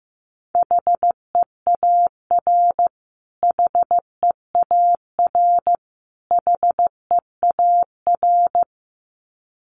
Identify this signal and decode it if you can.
Morse code, heard as a keyed tone, about 15 wpm, 700 Hz. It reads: HEAR HEAR HEAR
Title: Top 300 Words Repeated 15wpm